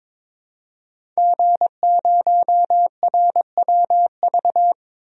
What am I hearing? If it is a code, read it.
Morse code, heard as a keyed tone, about 22 wpm, 700 Hz. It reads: G0RWV
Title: DXCC-100 Call Signs - 5 Characters - 22wpm